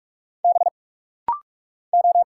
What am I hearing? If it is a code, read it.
Morse code, heard as a keyed tone, about 44 wpm, 700 Hz. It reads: B O
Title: Single Letter-Number Alt - Rapid-Fire 45wpm